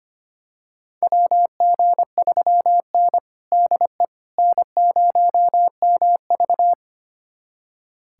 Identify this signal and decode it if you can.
Morse code, heard as a keyed tone, about 25 wpm, 700 Hz. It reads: WG3N DE N0MV
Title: USA Callsign DE Callsign - 4 Characters - 25wpm